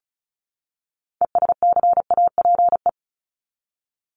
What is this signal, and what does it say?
Morse code, heard as a keyed tone, about 34 wpm, 700 Hz. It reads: ESCAPE
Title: Top 1501-1750 Words 35wpm